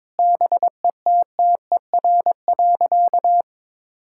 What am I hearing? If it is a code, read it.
Morse code, heard as a keyed tone, about 22 wpm, 700 Hz. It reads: BETTER.
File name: Difficult Sentences with Hints - 22wpm